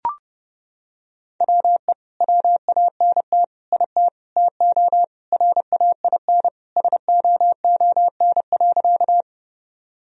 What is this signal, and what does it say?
Morse code, heard as a keyed tone, about 30 wpm, 700 Hz. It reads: WE WANT IT TO RAIN SOON.
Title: Sentences from Top 1000 Words - Review Part 1 - 30wpm